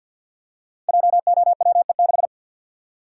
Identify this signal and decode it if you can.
Morse code, heard as a keyed tone, about 52 wpm, 690 Hz. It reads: JOWE6
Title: Code Groups - 5 Letter:Numbers - 50wpm